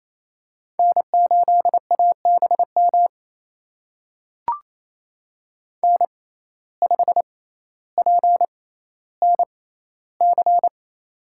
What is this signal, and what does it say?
Morse code, heard as a keyed tone, about 28 wpm, spaced as for 14 wpm, 700 Hz. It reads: N8ABM N5PNC
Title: USA Call Signs - 5 Characters - 6x Character Spacing - 28wpm